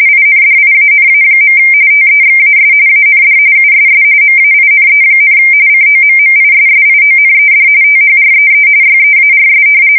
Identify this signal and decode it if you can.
RTTY, about 45.45 baud. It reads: INDIANA. TEXAS AND USS MAINE, COMMISSIONED THREE YEARS
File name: fsk170 / battleships